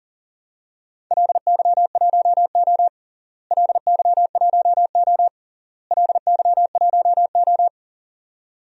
Morse code, heard as a keyed tone, about 37 wpm, 700 Hz. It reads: LY1O LY1O LY1O
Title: DXCC-100 Call Signs - 4 Characters - Repeated 3x - 40wpm